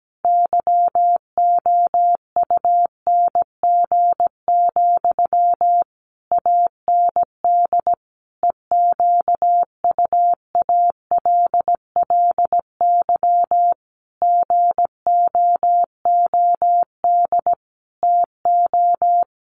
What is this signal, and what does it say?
Morse code, heard as a keyed tone, about 17 wpm, 700 Hz. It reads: YOUNG, AND EQUALLY GOOD TO
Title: Sentences from Top 1400 Words 17wpm